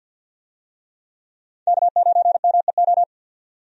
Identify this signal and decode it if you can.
Morse code, heard as a keyed tone, about 50 wpm, 700 Hz. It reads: K9MEO
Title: USA Call Signs - 5 Characters - 50wpm